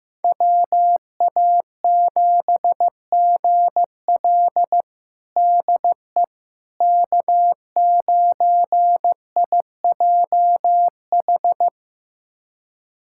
Morse code, heard as a keyed tone, about 16 wpm, 700 Hz. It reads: WA7GL DE K9IJH
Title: USA Callsign DE Callsign - 5 Characters - 15wpm